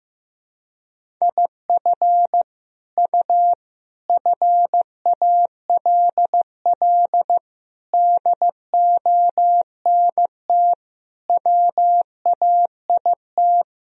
Morse code, heard as a keyed tone, about 15 wpm, 700 Hz. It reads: IF U FALL DONT WAIT
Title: Sentences from Top 500 Words and Abbreviations 15wpm